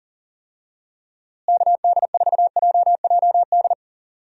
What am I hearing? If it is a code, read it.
Morse code, heard as a keyed tone, about 39 wpm, 700 Hz. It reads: KD4JJD
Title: USA Call Signs - 6 Characters - Repeated 2x - 40wpm